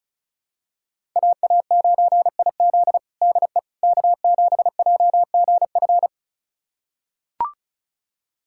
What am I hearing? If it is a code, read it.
Morse code, heard as a keyed tone, about 36 wpm, 700 Hz. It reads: AA9IZ DE K7JGF
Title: USA Callsign DE Callsign - 5 Characters - 35wpm